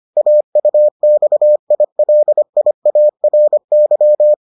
Morse code, heard as a keyed tone, about 25 wpm, 600 Hz. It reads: AUXILIARY